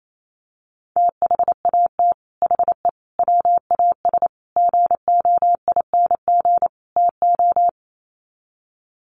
Morse code, heard as a keyed tone, about 28 wpm, 700 Hz. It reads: THAT HE WAS GOING TO
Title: Sets of 5 Words 28wpm